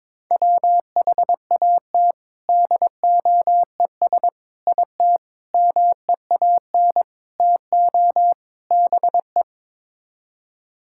Morse code, heard as a keyed tone, about 22 wpm, 700 Hz. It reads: WHAT DOES IT MEAN TO BE
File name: practice / 22wpm / Sets of 6 Words - 22wpm